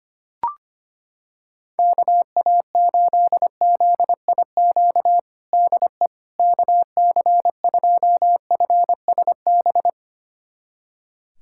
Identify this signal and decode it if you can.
Morse code, heard as a keyed tone, about 25 wpm, 700 Hz. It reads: KA8ZIQ DE KC2FSB
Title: USA Callsign DE Callsign 25wpm